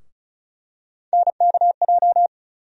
Morse code, heard as a keyed tone, about 36 wpm, 700 Hz. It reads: NKJ
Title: USA Call Sign Postfixes - 3 Letters 35wpm